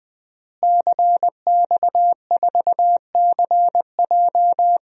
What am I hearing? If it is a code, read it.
Morse code, heard as a keyed tone, about 20 wpm, 700 Hz. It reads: CX4CJ
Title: DXCC-100 Call Signs - 5 Characters - Repeated 3x - 20wpm